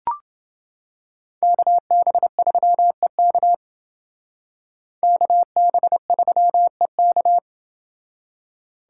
Morse code, heard as a keyed tone, about 28 wpm, 700 Hz. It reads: KB3EK KB3EK